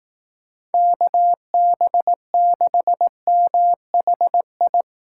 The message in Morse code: KB6MHI

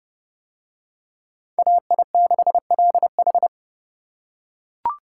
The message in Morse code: AI6LH

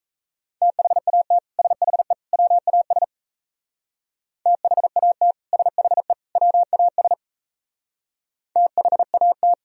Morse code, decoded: THAT SHE WAS THAT SHE WAS THAT

